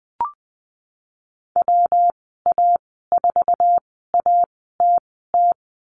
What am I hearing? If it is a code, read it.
Morse code, heard as a keyed tone, about 20 wpm, spaced as for 15 wpm, 700 Hz. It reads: WA4ATT
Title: 128 Lesson 21g - Number 6 - Call Signs 20wpm